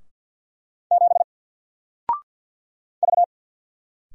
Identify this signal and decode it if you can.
Morse code, heard as a keyed tone, about 50 wpm, 700 Hz. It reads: 7 V
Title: Single Letter-Number 50wpm